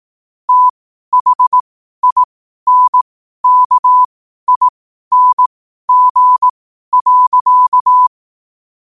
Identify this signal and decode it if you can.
Morse code, heard as a keyed tone, about 18 wpm, spaced as for 12 wpm, 1000 Hz. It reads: THINKING.
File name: qotdfiles / 12wpm